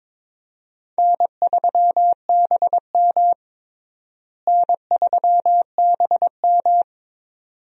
Morse code, heard as a keyed tone, about 22 wpm, 700 Hz. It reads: N3BM N3BM
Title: USA Call Signs - 4 Characters - Repeated 2x - 22wpm